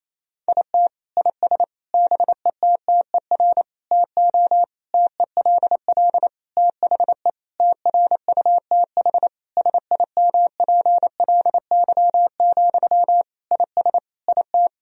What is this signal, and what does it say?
Morse code, heard as a keyed tone, about 28 wpm, 700 Hz. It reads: IT IS BETTER TO TELL THE TRUTH SIMPLY, IS IT
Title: Sentences from Top 1000 Words - Review Part 4 - 28wpm